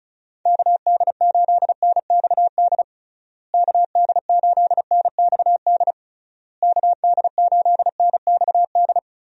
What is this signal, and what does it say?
Morse code, heard as a keyed tone, about 36 wpm, 700 Hz. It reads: KD8NXD KD8NXD KD8NXD
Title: USA Call Signs - 6 Characters - Repeated 3x - 35wpm